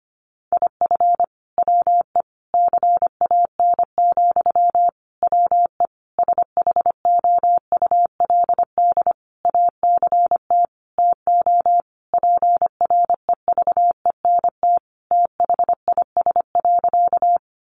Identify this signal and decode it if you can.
Morse code, heard as a keyed tone, about 25 wpm, 700 Hz. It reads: IF WE CAN, WE SHOULD ACT TO PREVENT THIS.
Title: Sentences from Top 1100 Words 25wpm